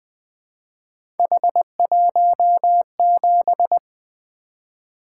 Morse code, heard as a keyed tone, about 20 wpm, 700 Hz. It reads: H17
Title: Three Letter-Number Combinations - 20wpm